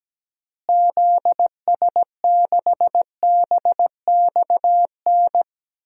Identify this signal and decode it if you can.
Morse code, heard as a keyed tone, about 17 wpm, 700 Hz. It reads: ZS6BXN
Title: DXCC-100 Call Signs - 6 Characters - Repeated 3x - 17wpm